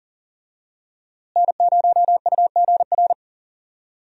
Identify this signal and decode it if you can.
Morse code, heard as a keyed tone, about 40 wpm, 700 Hz. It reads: N0UGR
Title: USA Call Signs - 5 Characters - Repeated 3x - 40wpm